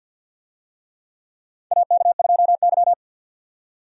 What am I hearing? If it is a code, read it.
Morse code, heard as a keyed tone, about 52 wpm, 700 Hz. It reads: AK2Y
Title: USA Call Signs - 4 Characters - 50wpm